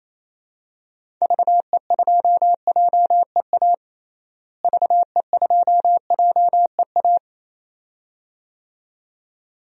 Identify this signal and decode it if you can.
Morse code, heard as a keyed tone, about 28 wpm, 700 Hz. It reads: VE2JEA VE2JEA
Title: DXCC-100 Call Signs - 6 Characters - Repeated 2x - 28wpm